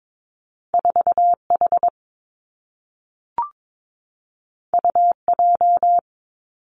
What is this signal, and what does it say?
Morse code, heard as a keyed tone, about 22 wpm, 700 Hz. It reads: 4 H U J
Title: Two Challenging Letter-Number Combinations 22wpm